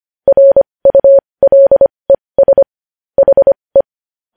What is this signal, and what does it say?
Morse code, heard as a keyed tone, about 25 wpm, spaced as for 20 wpm, 550 Hz. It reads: RULES HE
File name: podcast / News headlines-1x-25x20